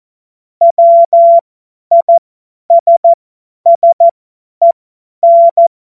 Morse code, heard as a keyed tone, about 14 wpm, spaced as for 10 wpm, 680 Hz. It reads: WISSEN
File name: Morsekurs / Witze / Witz 27. Juli 2021, 50 F70